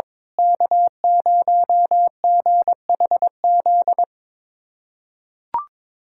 Morse code, heard as a keyed tone, about 22 wpm, 700 Hz. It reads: K0GHZ
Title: USA Callsign DE Callsign - 5 Characters - 22wpm